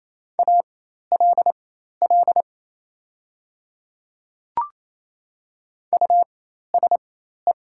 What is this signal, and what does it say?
Morse code, heard as a keyed tone, about 28 wpm, spaced as for 13 wpm, 700 Hz. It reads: ALL USE
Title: 4x Character Spacing - Top 100 Words - 28wpm